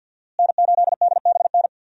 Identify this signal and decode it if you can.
Morse code, heard as a keyed tone, about 51 wpm, 700 Hz. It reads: N8DBN
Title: USA Call Signs - 5 Characters - Repeated 2x - 50wpm